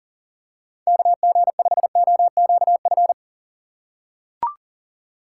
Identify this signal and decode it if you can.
Morse code, heard as a keyed tone, about 40 wpm, 700 Hz. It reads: KG5OQF
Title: USA Call Signs - 6 Characters - 3x Character Spacing - 40wpm